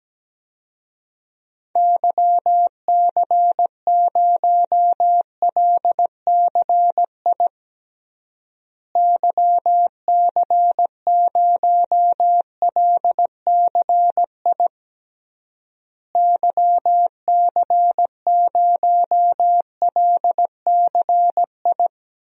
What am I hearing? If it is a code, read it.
Morse code, heard as a keyed tone, about 17 wpm, 700 Hz. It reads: YC0LCI YC0LCI YC0LCI
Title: DXCC-100 Call Signs - 6 Characters - Repeated 3x - 17wpm